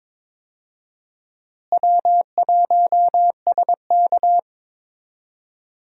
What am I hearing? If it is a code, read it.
Morse code, heard as a keyed tone, about 22 wpm, 700 Hz. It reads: W1SK
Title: USA Call Signs - 4 Characters - Repeated 3x - 22wpm